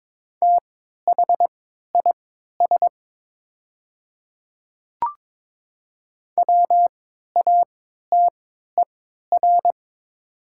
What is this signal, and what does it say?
Morse code, heard as a keyed tone, about 21 wpm, spaced as for 12 wpm, 700 Hz. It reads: THIS WATER
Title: 3x Character Spacing - Top 100 Words - 22wpm